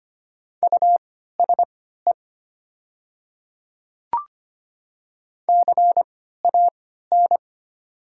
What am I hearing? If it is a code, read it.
Morse code, heard as a keyed tone, about 25 wpm, spaced as for 14 wpm, 700 Hz. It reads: USE CAN
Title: 3x Character Spacing - Top 100 Words - 25wpm